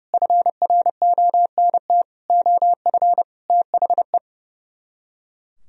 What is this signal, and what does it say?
Morse code, heard as a keyed tone, about 38 wpm, 700 Hz. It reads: FRONT OF THE
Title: Speed-Racing - Sets of 5 Words 30wpm